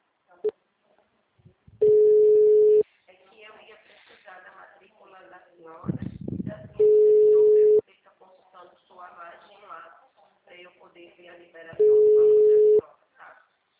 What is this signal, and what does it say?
Ringback tone (Europe, 425 Hz), with speech.
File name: out- -2001- - - .5930